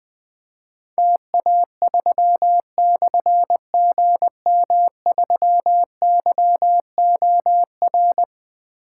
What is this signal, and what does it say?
Morse code, heard as a keyed tone, about 20 wpm, 700 Hz. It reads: TA3/GM3YOR
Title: Difficult Call Signs 20wpm